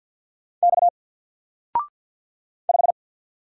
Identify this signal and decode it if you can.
Morse code, heard as a keyed tone, about 50 wpm, 700 Hz. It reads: X 5
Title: Single Letter-Number Alt 50wpm